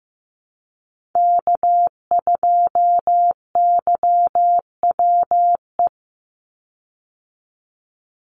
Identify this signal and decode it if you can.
Morse code, heard as a keyed tone, about 14 wpm, 700 Hz. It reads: K2YWE
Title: Call Signs Repeated 2x - 15wpm